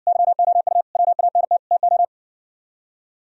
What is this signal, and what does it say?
Morse code, heard as a keyed tone, about 68 wpm, 700 Hz. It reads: YOU WANT TO